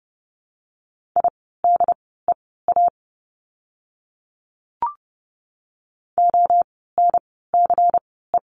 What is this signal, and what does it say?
Morse code, heard as a keyed tone, about 30 wpm, spaced as for 17 wpm, 700 Hz. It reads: IDEA ONCE
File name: course / 30:10wpm / 108 Lesson 18f - Letter m - Words 30wpm